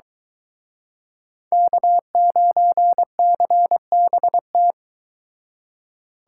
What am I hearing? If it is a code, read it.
Morse code, heard as a keyed tone, about 23 wpm, 700 Hz. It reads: K9CBT